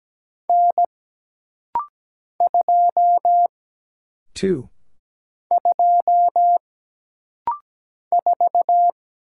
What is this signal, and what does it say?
Morse code, heard as a keyed tone, about 17 wpm, 700 Hz. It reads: N 2 2 4
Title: Single Letter-Number Alt - Rapid-Fire 17wpm